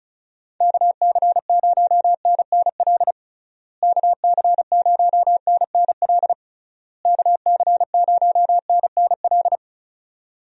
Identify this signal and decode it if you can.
Morse code, heard as a keyed tone, about 36 wpm, 700 Hz. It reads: KC0NNL KC0NNL KC0NNL
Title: USA Call Signs - 6 Characters - Repeated 3x - 35wpm